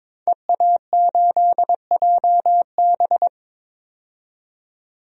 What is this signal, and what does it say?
Morse code, heard as a keyed tone, about 22 wpm, 700 Hz. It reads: EA8JB
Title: DXCC-100 Call Signs - 5 Characters - 22wpm